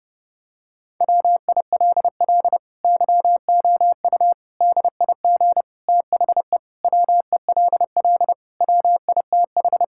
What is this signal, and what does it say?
Morse code, heard as a keyed tone, about 31 wpm, 700 Hz. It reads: WILL YOU DIG THE WELL WITH
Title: Sentences from Top 1600 Words 30wpm